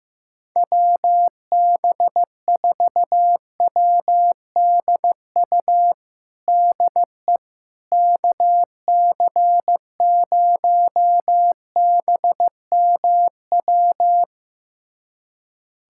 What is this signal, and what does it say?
Morse code, heard as a keyed tone, about 15 wpm, 700 Hz. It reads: WB4WDU DE KC0BMW
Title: USA Callsign DE Callsign - 6 Characters - 15wpm